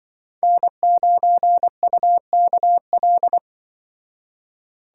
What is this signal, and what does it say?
Morse code, heard as a keyed tone, about 24 wpm, 700 Hz. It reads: N9UKL
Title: Speed Racing - USA 1x3 Call Signs 20wpm